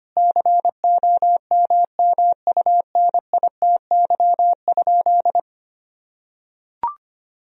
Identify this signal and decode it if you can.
Morse code, heard as a keyed tone, about 25 wpm, 700 Hz. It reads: COMMUNITY?
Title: Sentences from Top 1000 Words - Review Part 4 - 25wpm